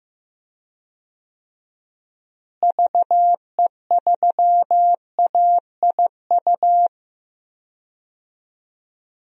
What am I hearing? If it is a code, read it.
Morse code, heard as a keyed tone, about 15 wpm, 700 Hz. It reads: VE3AIU
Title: DXCC-100 Call Signs - 6 Characters - Repeated 2x - 15wpm